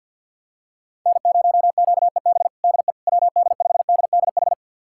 Morse code, heard as a keyed tone, about 47 wpm, 700 Hz. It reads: N0QEB DE WD5DDH
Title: USA Callsign DE Callsign 50wpm